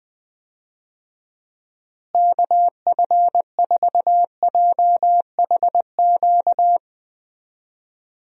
Morse code, heard as a keyed tone, about 20 wpm, 700 Hz. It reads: KF4JHQ
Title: USA Call Signs - 6 Characters - Repeated 3x - 20wpm